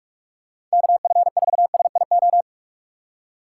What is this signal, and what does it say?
Morse code, heard as a keyed tone, about 46 wpm, 700 Hz. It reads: KF4SIO
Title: USA Call Signs - 6 Characters - 6x Character Spacing - 45wpm